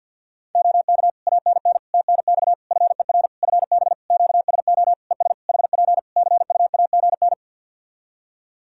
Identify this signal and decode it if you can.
Morse code, heard as a keyed tone, about 50 wpm, 700 Hz. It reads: OK ANN TNX FER FB QSO ES HP CUAGN
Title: Practice QSO - Exchange 4 - Ending - 50wpm